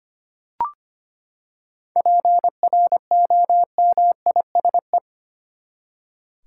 Morse code, heard as a keyed tone, about 25 wpm, 700 Hz. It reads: PROMISE
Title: Seven Letter Words - 25wpm